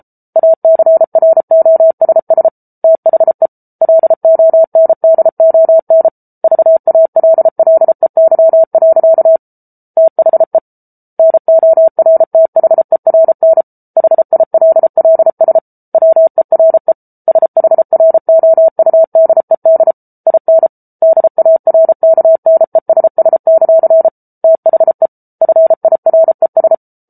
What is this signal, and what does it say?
Morse code, heard as a keyed tone, about 34 wpm, 650 Hz. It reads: ACROSS THE LONDON VALLEY. THE NORTHERN HILLS WERE SHROUDED IN DARKNESS; THE FIRES